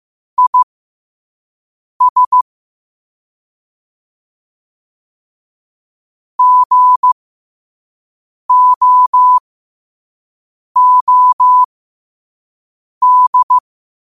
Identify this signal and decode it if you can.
Morse code, heard as a keyed tone, about 15 wpm, spaced as for 5 wpm, 1000 Hz. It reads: IS GOOD
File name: qotdfiles / 5wpm